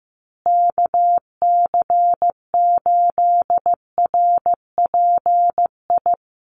Morse code, heard as a keyed tone, about 15 wpm, 700 Hz. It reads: KC8RPI
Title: USA Call Signs Repeated 2x 15wpm